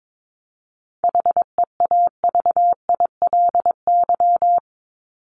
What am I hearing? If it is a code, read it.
Morse code, heard as a keyed tone, about 22 wpm, 700 Hz. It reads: HEAVILY
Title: LY Suffix - Top 100 Words with Base Words 22wpm